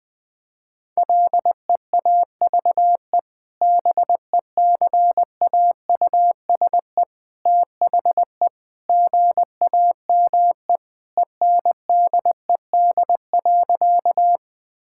Morse code, heard as a keyed tone, about 21 wpm, 700 Hz. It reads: LEAVE BECAUSE THE GAME ENDED.
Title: Sentences with 2-Word Phrases - 20wpm